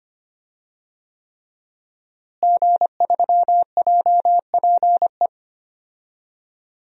Morse code, heard as a keyed tone, about 25 wpm, 700 Hz. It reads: G3JPE
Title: DXCC-100 Call Signs - 5 Characters - Repeated 3x - 25wpm